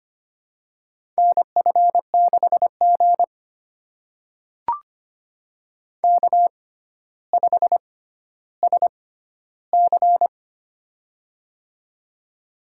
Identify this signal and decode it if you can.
Morse code, heard as a keyed tone, about 25 wpm, spaced as for 13 wpm, 700 Hz. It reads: NF6G K5SC
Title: USA Call Signs - 4 Characters - 6x Character Spacing - 25wpm